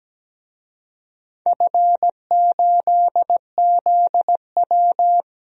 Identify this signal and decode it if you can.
Morse code, heard as a keyed tone, about 17 wpm, 700 Hz. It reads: F8ZW
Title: DXCC-100 Call Signs - 4 Characters - 17wpm